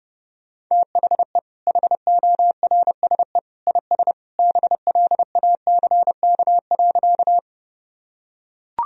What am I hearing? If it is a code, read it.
Morse code, heard as a keyed tone, about 32 wpm, 700 Hz. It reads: THE HORSE IS BLACK.
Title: Sentences from Top 1000 Words - Review Part 2 - 30wpm